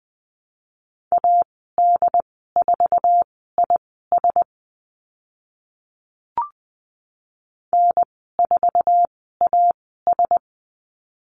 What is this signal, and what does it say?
Morse code, heard as a keyed tone, about 20 wpm, spaced as for 14 wpm, 700 Hz. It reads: AD4IS N4AS
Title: 082 Lesson 14g - Number 2 - Call Signs 20wpm